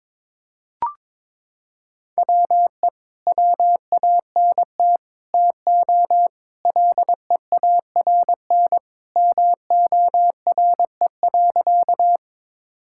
Morse code, heard as a keyed tone, about 22 wpm, 700 Hz. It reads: WE WANT TO LEARN MORE.
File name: practice / 22wpm / Sentences with 3-Word Phrases - 22wpm